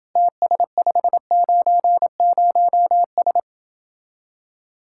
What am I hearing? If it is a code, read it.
Morse code, heard as a keyed tone, about 31 wpm, 700 Hz. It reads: TS590S